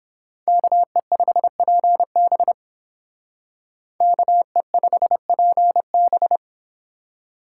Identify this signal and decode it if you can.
Morse code, heard as a keyed tone, about 31 wpm, 700 Hz. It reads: KE5PB KE5PB